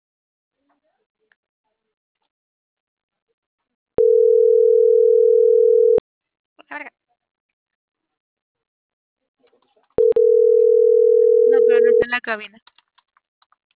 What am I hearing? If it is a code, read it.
Ringback tone (North America), 440 + 480 Hz.